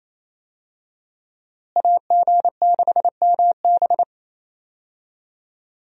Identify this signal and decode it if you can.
Morse code, heard as a keyed tone, about 28 wpm, 700 Hz. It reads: AG6MB
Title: USA Call Signs - 5 Characters - Repeated 2x - 28wpm